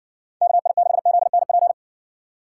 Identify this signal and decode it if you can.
Morse code, heard as a keyed tone, about 70 wpm, 700 Hz. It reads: KI6ZNP